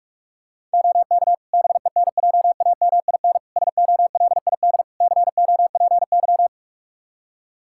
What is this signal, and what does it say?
Morse code, heard as a keyed tone, about 46 wpm, 700 Hz. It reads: OK BENJAMIN SOLID COPY
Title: Practice QSO - Exchange 3 - More Info - 45wpm